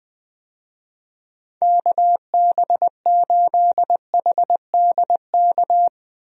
Morse code, heard as a keyed tone, about 20 wpm, 700 Hz. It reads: KB8HDK
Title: USA Call Signs - 6 Characters - 20wpm